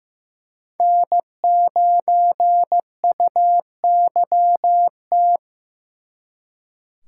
Morse code, heard as a keyed tone, about 19 wpm, 700 Hz. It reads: N9UYT